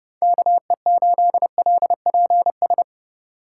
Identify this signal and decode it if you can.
Morse code, heard as a keyed tone, about 34 wpm, 700 Hz. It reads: KE8LPS